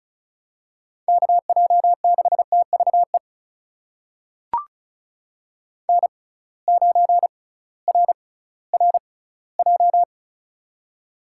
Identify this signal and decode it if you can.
Morse code, heard as a keyed tone, about 36 wpm, spaced as for 19 wpm, 700 Hz. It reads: KJ6TVE N9RRJ
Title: USA Call Signs - 6x Character Spacing - 35wpm